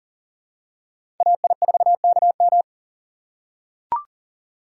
Morse code, heard as a keyed tone, about 38 wpm, 700 Hz. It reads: AI4KM
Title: USA Call Signs - 5 Characters - 2x Character Spacing - 40wpm